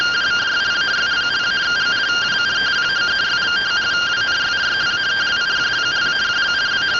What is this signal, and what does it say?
Two-tone teleprinter signal, about 170 Hz shift.